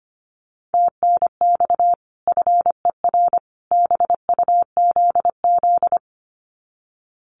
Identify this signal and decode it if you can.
Morse code, heard as a keyed tone, about 25 wpm, 700 Hz. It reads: TNX FER BUZZ
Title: Common QSO Phrases 25wpm